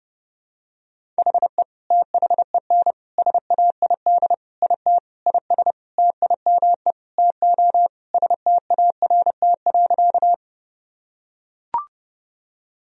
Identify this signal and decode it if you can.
Morse code, heard as a keyed tone, about 30 wpm, 700 Hz. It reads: HE THEN SAID IT IS TIME TO START.
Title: Sentences from Top 1000 Words - Review Part 1 - 30wpm